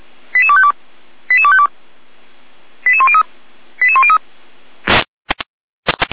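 Five-tone selective call (ZVEI), heard as quick runs of five tones, tones of about 70 ms. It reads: 88252 88252 88103 88103
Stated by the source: Received on an SDR; ZVEI five-tone selective call.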